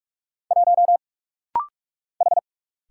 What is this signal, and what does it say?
Morse code, heard as a keyed tone, about 46 wpm, 700 Hz. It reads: 1 H
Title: Single Letter-Number Alt - Rapid-Fire 45wpm